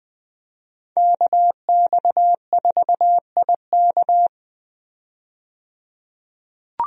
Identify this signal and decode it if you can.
Morse code, heard as a keyed tone, about 20 wpm, 700 Hz. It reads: KX4IK
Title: USA Call Signs - 5 Characters - 20wpm